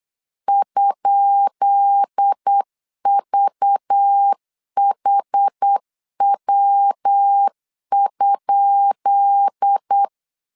Morse code, heard as a keyed tone, about 8 wpm, 800 Hz. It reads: ?VHW?
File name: LEZIONI / 10 SIGNS / 10B INT 8wpm 3534